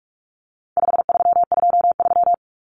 Morse code, heard as a keyed tone, about 44 wpm, 700 Hz. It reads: 5323
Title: Numbers - 4-Digit 45wpm